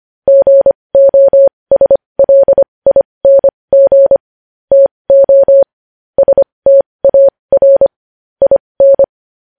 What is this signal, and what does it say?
Morse code, heard as a keyed tone, about 25 wpm, spaced as for 20 wpm, 550 Hz. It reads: GOSLING TO STAR IN
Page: news-headlines-in-morse-code-at-20-wpm